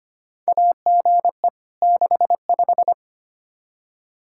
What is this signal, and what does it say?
Morse code, heard as a keyed tone, about 25 wpm, 700 Hz. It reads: AGE 65
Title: K1USN SST QSO Practice 25wpm